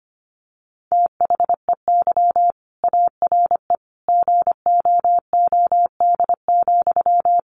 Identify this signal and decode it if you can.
Morse code, heard as a keyed tone, about 25 wpm, 700 Hz. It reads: THEY ARE GOOD,
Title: Sentences from Top 1100 Words 25wpm